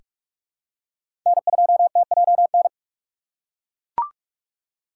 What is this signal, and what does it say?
Morse code, heard as a keyed tone, about 42 wpm, 700 Hz. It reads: N2TJN
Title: USA Call Signs - 5 Characters - 4x Character Spacing - 45wpm